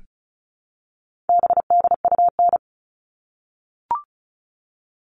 Morse code, heard as a keyed tone, about 34 wpm, 700 Hz. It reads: BDUN
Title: All Cut Numbers - 4-Digit 35wpm